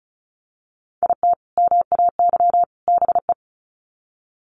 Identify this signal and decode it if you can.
Morse code, heard as a keyed tone, about 34 wpm, 700 Hz. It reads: IT MAY BE
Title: Sets of 3 Words 35wpm